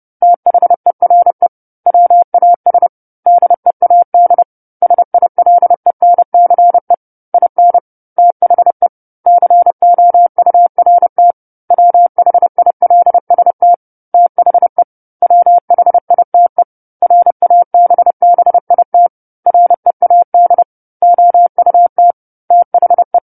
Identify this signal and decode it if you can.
Morse code, heard as a keyed tone, about 30 wpm, 700 Hz. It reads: THERE WAS DEAD SILENCE IN THE COURT WHILST THE WHITE RABBIT READ OUT THE
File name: 10-Chapter-10 30 WPM